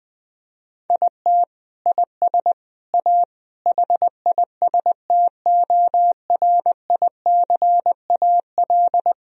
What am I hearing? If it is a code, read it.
Morse code, heard as a keyed tone, about 19 wpm, 700 Hz. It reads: IT IS A HISTORICAL
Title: Sentences from Top 1000 Words - Review Part 4 - 20wpm